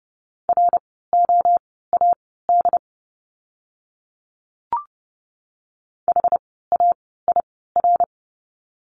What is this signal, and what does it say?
Morse code, heard as a keyed tone, about 30 wpm, spaced as for 17 wpm, 700 Hz. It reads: ROAD HAIR
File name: Lesson 17f - Letter u - Words 30wpm